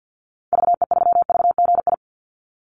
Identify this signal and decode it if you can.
Morse code, heard as a keyed tone, about 48 wpm, 700 Hz. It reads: VE3VGI
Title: Call Signs Repeated 3x - 50wpm